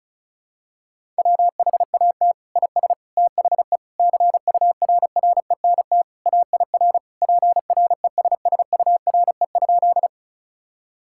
Morse code, heard as a keyed tone, about 34 wpm, 700 Hz. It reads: WHAT IS THE CURRENT AIR PRESSURE?
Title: Sentences from Top 1000 Words - Review Part 6 - 35wpm